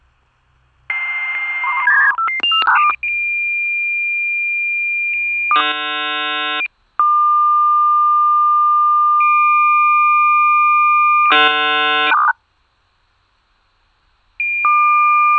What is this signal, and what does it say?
Dial-up modem handshake.